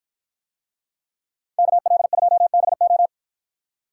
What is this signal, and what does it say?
Morse code, heard as a keyed tone, about 53 wpm, 700 Hz. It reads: KL2BO